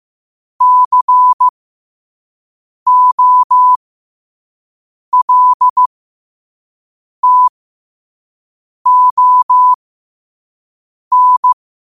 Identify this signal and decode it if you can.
Morse code, heard as a keyed tone, about 15 wpm, spaced as for 5 wpm, 1000 Hz. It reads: COLTON
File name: qotdfiles / 5wpm